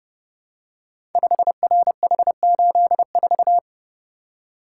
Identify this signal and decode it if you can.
Morse code, heard as a keyed tone, about 29 wpm, 700 Hz. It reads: 5RH84
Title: Code Groups - 5 Letter:Numbers - 30wpm